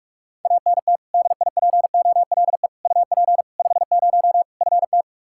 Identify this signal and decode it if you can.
Morse code, heard as a keyed tone, about 46 wpm, 700 Hz. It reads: ANT DIPOLE UP 50 FT
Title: Practice QSO - Exchange 2 - Info - 45wpm